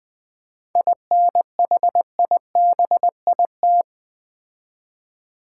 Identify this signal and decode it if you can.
Morse code, heard as a keyed tone, about 20 wpm, 700 Hz. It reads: INHIBIT
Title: 7 Letter Words - First 3 Letters Given - 20wpm